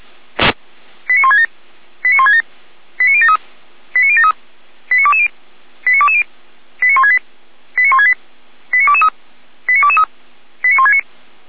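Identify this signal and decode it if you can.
Five-tone selective call (ZVEI), heard as quick runs of five tones, tones of about 70 ms. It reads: 89167 89167 89073 89073 89220 89220 89168 89168 89223 89223 89169
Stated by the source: ZVEI five-tone selective call received by SDR, from a Feuerwehr Orlamünde fire brigade server.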